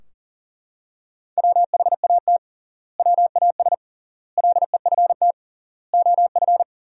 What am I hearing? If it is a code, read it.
Morse code, heard as a keyed tone, about 40 wpm, 700 Hz. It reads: WHAT WAS LEFT OF
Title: 3x Word-Spacing - Sentences from Top 300 Words - 40wpm